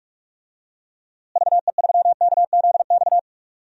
Morse code, heard as a keyed tone, about 47 wpm, 700 Hz. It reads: VE3KZX
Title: DXCC-100 Call Signs - 6 Characters - Repeated 3x - 45wpm